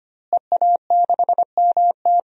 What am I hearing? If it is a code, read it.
Morse code, heard as a keyed tone, about 25 wpm, 700 Hz. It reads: EA6MT
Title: DXCC-100 Call Signs - 5 Characters - Repeated 2x - 25wpm